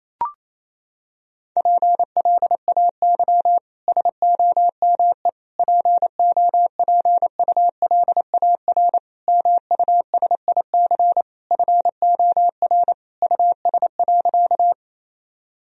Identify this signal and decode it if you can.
Morse code, heard as a keyed tone, about 28 wpm, 700 Hz. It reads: PLAY SOME POPULAR MUSIC FOR US.
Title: Sentences from Top 1000 Words - Review Part 3 - 28wpm